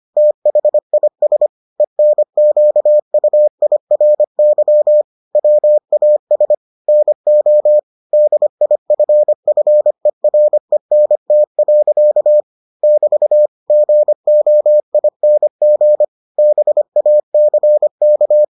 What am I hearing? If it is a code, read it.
Morse code, heard as a keyed tone, about 25 wpm, 600 Hz. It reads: THIS ENQUIRY WAS NO DIFFERENT. <BT> GOING BACK